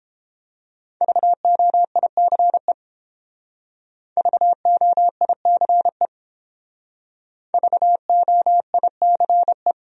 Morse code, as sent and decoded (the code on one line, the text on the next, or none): ...- --- .. -.-. / ...- --- .. -.-. / ...- --- .. -.-. .
VOICE VOICE VOICE